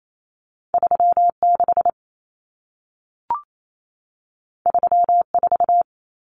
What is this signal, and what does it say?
Morse code, heard as a keyed tone, about 28 wpm, 700 Hz. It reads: 36 34
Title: Numbers - 2-Digit 28wpm